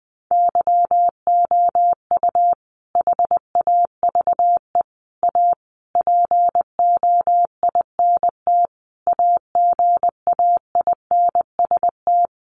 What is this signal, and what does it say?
Morse code, heard as a keyed tone, about 20 wpm, 700 Hz. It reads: YOU HAVE A POINT AGAINST
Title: Sentences from Top 1400 Words 20wpm